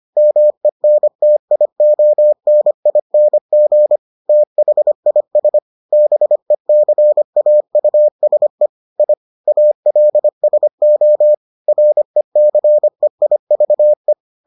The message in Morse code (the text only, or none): MENTIONING THIS BECAUSE I ALSO RECEIVE